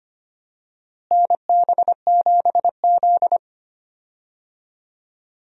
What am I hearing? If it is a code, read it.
Morse code, heard as a keyed tone, about 25 wpm, 700 Hz. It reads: NB7Z
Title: USA Call Signs - 4 Characters - Repeated 2x - 25wpm